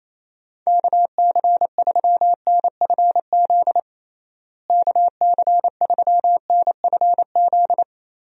Morse code, heard as a keyed tone, about 28 wpm, 700 Hz. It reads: KC3NFZ KC3NFZ
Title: USA Call Signs - 6 Characters - Repeated 3x - 28wpm